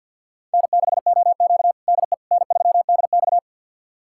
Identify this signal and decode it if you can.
Morse code, heard as a keyed tone, about 52 wpm, 700 Hz. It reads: N6OQ DE N3DX
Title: USA Callsign DE Callsign - 4 Characters - 50wpm